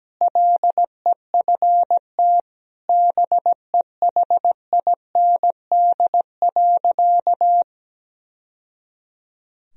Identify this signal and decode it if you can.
Morse code, heard as a keyed tone, about 17 wpm, 700 Hz. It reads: LEFT BEHIND.
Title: Sentences from Top 1000 Words - Review Part 2 - 17wpm